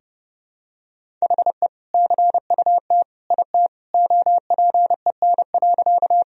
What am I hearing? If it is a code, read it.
Morse code, heard as a keyed tone, about 30 wpm, 700 Hz. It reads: HE CUT IT OPEN.
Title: Sentences from Top 1000 Words - Review Part 1 - 30wpm